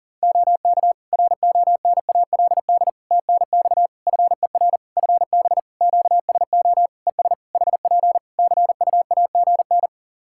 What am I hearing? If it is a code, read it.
Morse code, heard as a keyed tone, about 37 wpm, 700 Hz. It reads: OK RONALD TNX FER FB QSO ES HP CUAGN
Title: Practice QSO - Exchange 4 - Ending - 40wpm